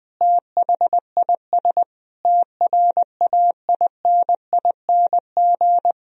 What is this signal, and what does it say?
Morse code, heard as a keyed tone, about 20 wpm, 700 Hz. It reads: THIS TRAINING
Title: Sentences from Top 1000 Words - Review Part 4 - 20wpm